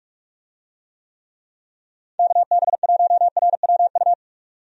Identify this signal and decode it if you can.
Morse code, heard as a keyed tone, about 47 wpm, 700 Hz. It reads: KB1RWU
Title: USA Call Signs - 6 Characters - 45wpm